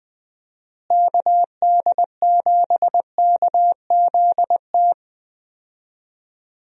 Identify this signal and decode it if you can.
Morse code, heard as a keyed tone, about 20 wpm, 700 Hz. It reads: KD7KZT
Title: USA Call Signs - 6 Characters - Repeated 2x - 20wpm